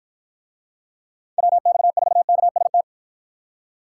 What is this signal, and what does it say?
Morse code, heard as a keyed tone, about 52 wpm, 700 Hz. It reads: WX4KST